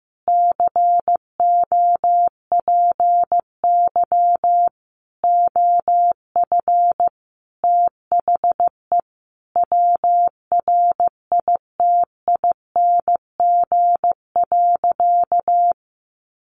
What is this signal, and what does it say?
Morse code, heard as a keyed tone, about 15 wpm, 700 Hz. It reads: COPY OF THE WRITING.
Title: Sentences from Top 1400 Words 15wpm